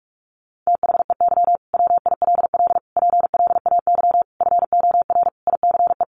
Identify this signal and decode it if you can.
Morse code, heard as a keyed tone, about 44 wpm, 700 Hz. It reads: THEY WILL PLAY FOR ICE
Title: Sentences from Top 1500 Words 45wpm